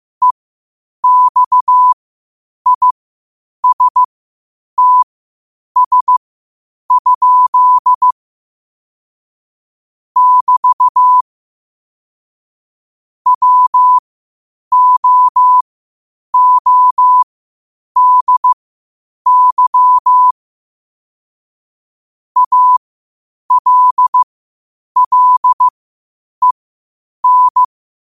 Morse code, heard as a keyed tone, about 15 wpm, spaced as for 8 wpm, 1000 Hz. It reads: EXISTS? <BT> WOODY ALLEN
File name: qotdfiles / 8wpm